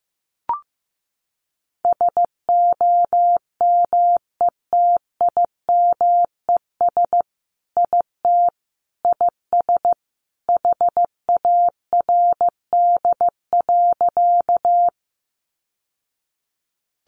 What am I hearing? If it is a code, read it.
Morse code, heard as a keyed tone, about 15 wpm, 700 Hz. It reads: SOMETIMES IT IS HARD.
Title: Sentences from Top 600 Words 15wpm